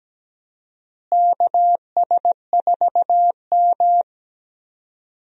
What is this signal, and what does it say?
Morse code, heard as a keyed tone, about 17 wpm, 700 Hz. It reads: KS4M
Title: USA Call Signs - 4 Characters - Repeated 3x - 17wpm